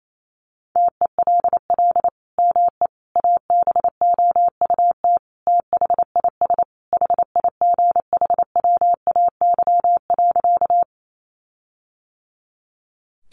Morse code, heard as a keyed tone, about 28 wpm, 700 Hz. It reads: TELL ME ABOUT THIS HIGHWAY.
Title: Sentences from Top 1400 Words 28wpm